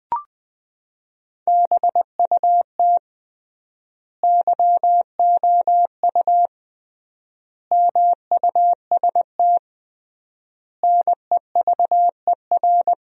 Morse code, heard as a keyed tone, about 20 wpm, 700 Hz. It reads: BUT YOU MUST NEVER
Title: 3x Word-Spacing - Sentences from Top 300 Words - 20wpm